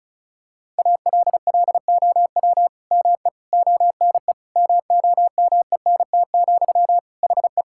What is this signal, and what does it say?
Morse code, heard as a keyed tone, about 36 wpm, 700 Hz. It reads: ALLOW ME ONE MOMENT, HE
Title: Sentences from Top 1000 Words - Review Part 5 - 35wpm